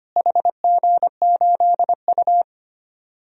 Morse code, heard as a keyed tone, about 25 wpm, 700 Hz. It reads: HG8U
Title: DXCC-100 Call Signs - 4 Characters - Repeated 3x - 25wpm